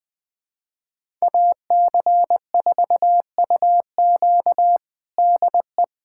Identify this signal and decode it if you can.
Morse code, heard as a keyed tone, about 20 wpm, 700 Hz. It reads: AC4UQ DE
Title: USA Callsign DE Callsign - 5 Characters - 20wpm